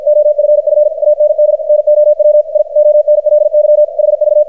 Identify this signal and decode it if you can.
Morse code, heard as a keyed tone, about 52 wpm, 600 Hz, with radio noise. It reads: OFF AND NOW I ONLY HV
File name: archived / CRSnet